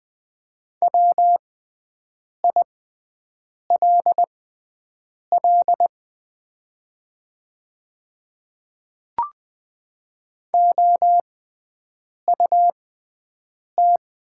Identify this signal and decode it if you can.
Morse code, heard as a keyed tone, about 20 wpm, spaced as for 7 wpm, 700 Hz. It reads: WILL OUT
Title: 6x Character Spacing - Top 100 Words - 20wpm